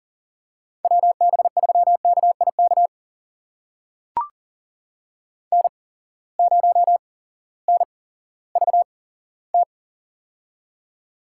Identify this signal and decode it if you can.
Morse code, heard as a keyed tone, about 39 wpm, spaced as for 18 wpm, 700 Hz. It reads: WB3KIK N0NVT
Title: USA Call Signs - 8x Character Spacing - 40wpm